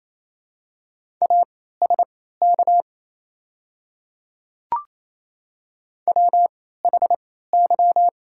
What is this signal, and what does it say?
Morse code, heard as a keyed tone, about 28 wpm, spaced as for 16 wpm, 690 Hz. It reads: ASK WHY
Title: 3x Character Spacing - Top 200 Words - 28wpm